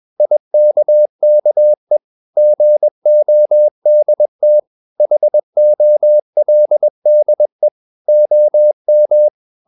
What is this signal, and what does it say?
Morse code, heard as a keyed tone, about 21 wpm, 600 Hz. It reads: IKKE GODT HOLDE OM
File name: Ferie QRS 21wpm